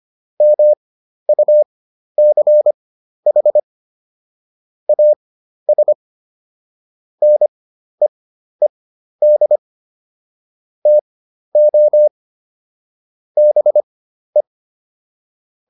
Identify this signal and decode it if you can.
Morse code, heard as a keyed tone, about 25 wpm, spaced as for 12 wpm, 600 Hz. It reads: MUCH AS NEED TO BE